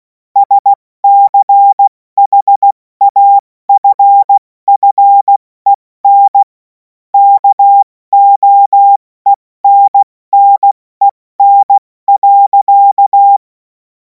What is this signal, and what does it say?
Morse code, heard as a keyed tone, about 16 wpm, 800 Hz. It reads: SCHAFFEN KOENNEN.
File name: Morsekurs / Witze / Witz 24. Februar 2026 70 F80